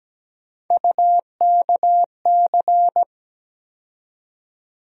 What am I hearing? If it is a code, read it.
Morse code, heard as a keyed tone, about 17 wpm, 700 Hz. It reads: UKC
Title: USA Call Sign Postfixes - 3 Letters 17wpm